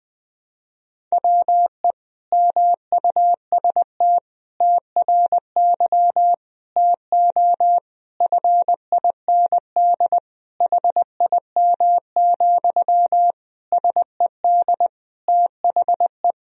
Morse code, heard as a keyed tone, about 21 wpm, 700 Hz. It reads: WE MUST TRY TO FIND HIM, SED THE
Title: Sentences from Top 400 Words and Abbreviations 20wpm